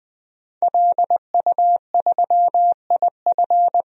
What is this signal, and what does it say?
Morse code, heard as a keyed tone, about 21 wpm, 700 Hz. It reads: LU3IF
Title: DXCC-100 Call Signs - 5 Characters - 20wpm